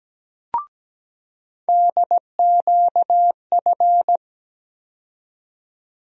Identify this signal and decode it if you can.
Morse code, heard as a keyed tone, about 17 wpm, 700 Hz. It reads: DQF
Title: USA Call Sign Postfixes - 3 Letters 17wpm